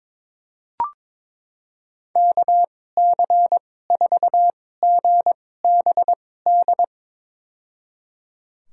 Morse code, heard as a keyed tone, about 21 wpm, spaced as for 16 wpm, 700 Hz. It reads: KC4GBD
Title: USA Call Signs - 2x Character Spacing - 22wpm